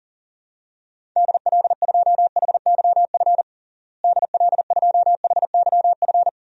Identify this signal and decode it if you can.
Morse code, heard as a keyed tone, about 40 wpm, 700 Hz. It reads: DL2HYF DL2HYF
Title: DXCC-100 Call Signs - 6 Characters - Repeated 2x - 40wpm